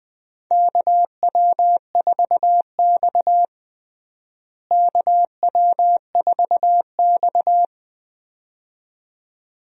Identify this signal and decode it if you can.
Morse code, heard as a keyed tone, about 20 wpm, 700 Hz. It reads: KW4X KW4X
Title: USA Call Signs - 4 Characters - Repeated 2x - 20wpm